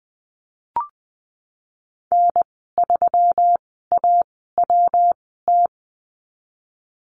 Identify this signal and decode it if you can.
Morse code, heard as a keyed tone, about 20 wpm, spaced as for 15 wpm, 700 Hz. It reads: N3AWT
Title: 173 Lesson 28g - Number 9 - Call Signs 20wpm